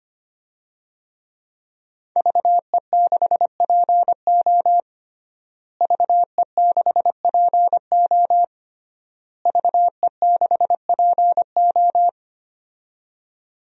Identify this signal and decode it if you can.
Morse code, heard as a keyed tone, about 25 wpm, 700 Hz. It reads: VE6PO VE6PO VE6PO
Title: DXCC-100 Call Signs - 5 Characters - Repeated 3x - 25wpm